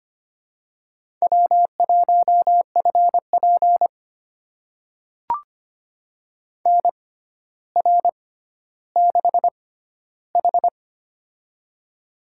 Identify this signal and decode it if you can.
Morse code, heard as a keyed tone, about 25 wpm, spaced as for 13 wpm, 700 Hz. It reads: W1FP NR6H
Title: USA Call Signs - 4 Characters - 6x Character Spacing - 25wpm